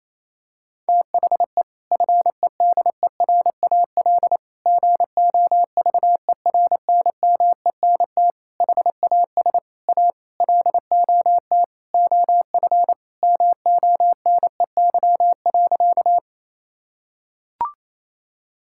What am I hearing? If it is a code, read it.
Morse code, heard as a keyed tone, about 28 wpm, 700 Hz. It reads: THE FEDERAL GOVERNMENT HAS A LOT OF MONEY.
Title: Sentences from Top 1000 Words - Review Part 5 - 28wpm